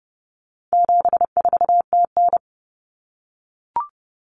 Morse code, heard as a keyed tone, about 30 wpm, 700 Hz. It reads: 74TN
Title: Common Cut Numbers - 4-Digit 30wpm